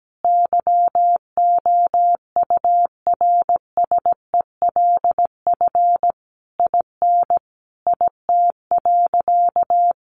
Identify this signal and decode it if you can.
Morse code, heard as a keyed tone, about 17 wpm, 700 Hz. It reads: YOURSELF IN IT.
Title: Sentences from Top 1300 Words 17wpm